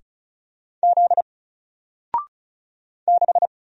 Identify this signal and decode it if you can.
Morse code, heard as a keyed tone, about 36 wpm, 700 Hz. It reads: Z 6
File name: Single-Letter-Number-35wpm